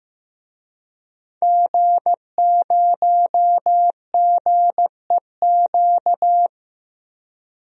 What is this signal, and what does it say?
Morse code, heard as a keyed tone, about 15 wpm, 700 Hz. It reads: G0GEQ
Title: DXCC-100 Call Signs - 5 Characters - 15wpm